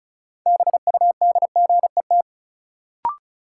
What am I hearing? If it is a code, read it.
Morse code, heard as a keyed tone, about 36 wpm, 700 Hz. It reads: BUDGET
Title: Sentences from Top 1200 Words - New Word Drill Alt - 35wpm